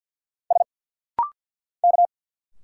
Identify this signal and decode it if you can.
Morse code, heard as a keyed tone, about 50 wpm, 700 Hz. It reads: S K
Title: Single Letters - Rapid-Fire 50wpm